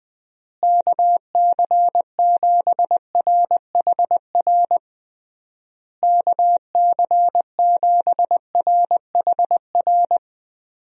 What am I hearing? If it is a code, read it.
Morse code, heard as a keyed tone, about 20 wpm, 700 Hz. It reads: KC7RHR KC7RHR
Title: USA Call Signs - 6 Characters - Repeated 3x - 20wpm